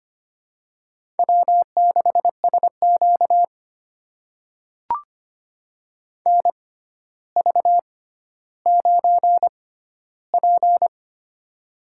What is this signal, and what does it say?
Morse code, heard as a keyed tone, about 25 wpm, spaced as for 13 wpm, 700 Hz. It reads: W6SQ NV9P
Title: USA Call Signs - 4 Characters - 6x Character Spacing - 25wpm